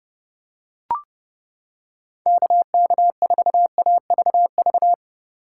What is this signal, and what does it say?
Morse code, heard as a keyed tone, about 31 wpm, 700 Hz. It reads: KK4AVV